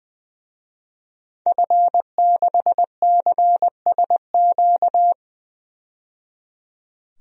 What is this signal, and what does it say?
Morse code, heard as a keyed tone, about 21 wpm, 700 Hz. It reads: F6CSQ
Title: DXCC-100 Call Signs - 5 Characters - 20wpm